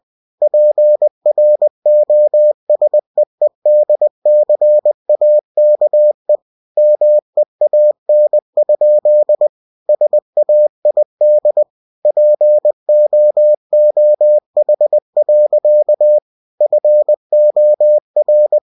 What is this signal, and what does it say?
Morse code, heard as a keyed tone, about 20 wpm, 600 Hz. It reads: PROSEEDCAKE MEAN? SAID POOH. FOR
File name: cw-books / winnie-the-pooh / winnie-20wpm-page-0029